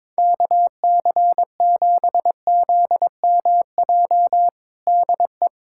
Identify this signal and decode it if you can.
Morse code, heard as a keyed tone, about 22 wpm, 700 Hz. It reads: KC7ZMJ DE
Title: USA Callsign DE Callsign - 6 Characters - 22wpm